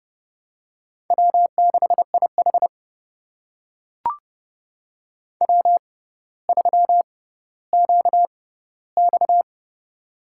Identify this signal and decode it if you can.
Morse code, heard as a keyed tone, about 31 wpm, spaced as for 15 wpm, 700 Hz. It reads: W6IH W3QX
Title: USA Call Signs - 4 Characters - 6x Character Spacing - 30wpm